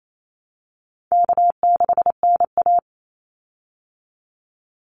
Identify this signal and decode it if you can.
Morse code, heard as a keyed tone, about 28 wpm, 700 Hz. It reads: K6NA
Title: Easy Call Signs Repeated 3x - 28wpm